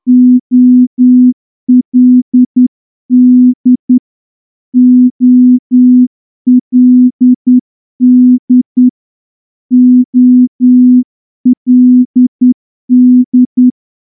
Morse code, heard as a keyed tone, about 10 wpm, 250 Hz. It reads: OLD OLD OLD
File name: morse-training-10WPM